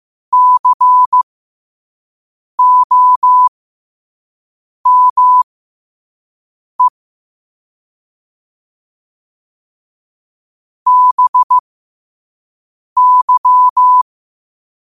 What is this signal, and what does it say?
Morse code, heard as a keyed tone, about 15 wpm, spaced as for 5 wpm, 1000 Hz. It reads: COME BY
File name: qotdfiles / 5wpm